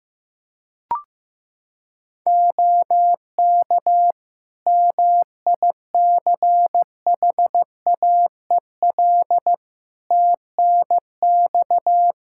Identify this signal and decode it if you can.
Morse code, heard as a keyed tone, about 15 wpm, 700 Hz. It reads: OK MICHAEL TNX
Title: K1USN SST QSO Practice 15wpm